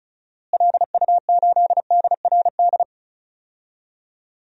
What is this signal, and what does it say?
Morse code, heard as a keyed tone, about 34 wpm, 700 Hz. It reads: LU8DRD
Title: DXCC-100 Call Signs - 6 Characters - Repeated 3x - 35wpm